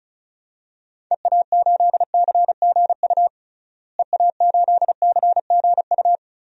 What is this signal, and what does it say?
Morse code, heard as a keyed tone, about 34 wpm, 700 Hz. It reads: EA8CGU EA8CGU
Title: DXCC-100 Call Signs - 6 Characters - Repeated 2x - 35wpm